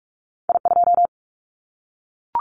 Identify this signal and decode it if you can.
Morse code, heard as a keyed tone, about 44 wpm, 700 Hz. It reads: I2
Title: Callsign Prefixes for Top 50 DXCC - 45wpm